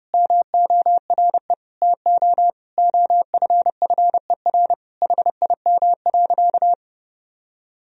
Morse code, heard as a keyed tone, about 30 wpm, 700 Hz. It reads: MORE TO OFFER HIM.
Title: Sentences from Top 1000 Words - Review Part 4 - 30wpm